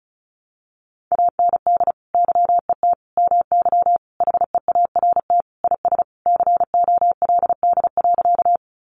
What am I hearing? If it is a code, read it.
Morse code, heard as a keyed tone, about 34 wpm, 700 Hz. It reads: AND YET MY HEART IS COLD.
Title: Sentences from Top 1200 Words 35wpm